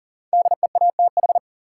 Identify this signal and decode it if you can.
Morse code, heard as a keyed tone, about 40 wpm, 700 Hz. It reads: DEATH
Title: Sentences from Top 1100 Words - New Word Drill - 40wpm